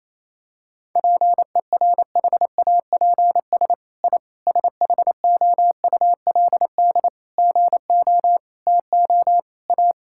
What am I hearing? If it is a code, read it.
Morse code, heard as a keyed tone, about 28 wpm, 700 Hz. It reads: PERHAPS I SHOULD GO TO A
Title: Sentences from Top 1000 Words - Review Part 5 - 28wpm